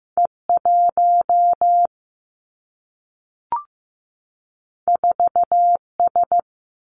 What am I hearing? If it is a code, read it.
Morse code, heard as a keyed tone, about 15 wpm, 700 Hz. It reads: E 1 4 S
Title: Two Challenging Letter-Number Combinations 15wpm